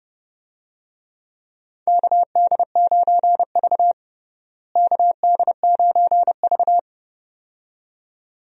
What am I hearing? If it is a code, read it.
Morse code, heard as a keyed tone, about 32 wpm, 700 Hz. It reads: KD9V KD9V
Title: USA Call Signs - 4 Characters - Repeated 2x - 30wpm